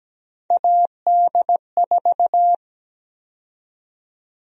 Morse code, heard as a keyed tone, about 17 wpm, 700 Hz. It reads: AD4
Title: USA Call Sign Prefixes - 17wpm